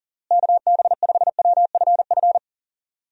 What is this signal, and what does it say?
Morse code, heard as a keyed tone, about 40 wpm, 700 Hz. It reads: KB5WFF
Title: USA Call Signs - 6 Characters - Repeated 3x - 40wpm